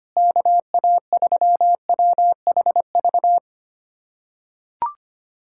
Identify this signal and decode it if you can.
Morse code, heard as a keyed tone, about 25 wpm, 700 Hz. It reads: KA3WHV
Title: USA Callsign DE Callsign - 6 Characters - 25wpm